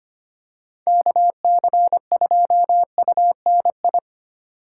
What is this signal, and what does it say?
Morse code, heard as a keyed tone, about 29 wpm, 700 Hz. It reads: KC2UNI
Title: Speed Racing - USA 2x3 Call Signs 25wpm